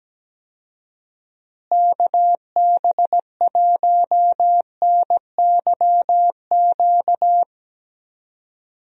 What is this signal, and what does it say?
Morse code, heard as a keyed tone, about 17 wpm, 700 Hz. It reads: KB1NYQ
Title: USA Call Signs - 6 Characters - Repeated 2x - 17wpm